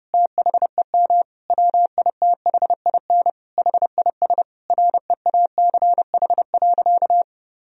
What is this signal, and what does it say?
Morse code, heard as a keyed tone, about 30 wpm, 700 Hz. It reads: THEM WITHIN HIS REACH.
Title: Sentences from Top 1000 Words - Review Part 4 - 30wpm